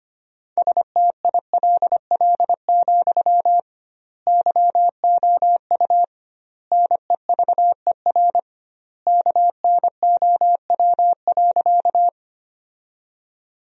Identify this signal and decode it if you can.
Morse code, heard as a keyed tone, about 25 wpm, 700 Hz. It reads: STILL, YOU NEVER KNOW.
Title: 2x Word-Spacing - Sentences from Top 300 Words - 25wpm